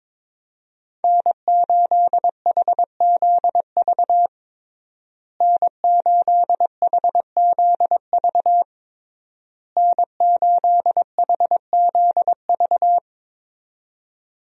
Morse code, heard as a keyed tone, about 22 wpm, 700 Hz. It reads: N8HZV N8HZV N8HZV
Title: USA Call Signs - 5 Characters - Repeated 3x - 22wpm